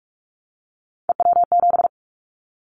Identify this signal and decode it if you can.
Morse code, heard as a keyed tone, about 44 wpm, 700 Hz. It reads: EW7
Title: Callsign Prefixes for Top 100 DXCC - 45wpm